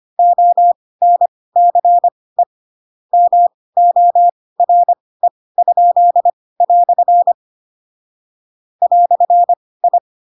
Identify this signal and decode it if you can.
Morse code, heard as a keyed tone, about 25 wpm, spaced as for 18 wpm, 700 Hz. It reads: ONCE MORE?" "I